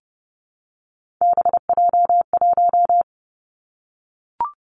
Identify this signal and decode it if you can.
Morse code, heard as a keyed tone, about 30 wpm, 700 Hz. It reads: BJ1
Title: Three Challenging Letter-Number Combinations 30wpm